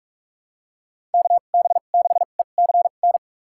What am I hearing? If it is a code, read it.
Morse code, heard as a keyed tone, about 46 wpm, spaced as for 33 wpm, 700 Hz. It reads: KB6ECN
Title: USA Call Signs - 6 Characters - 2x Character Spacing - 45wpm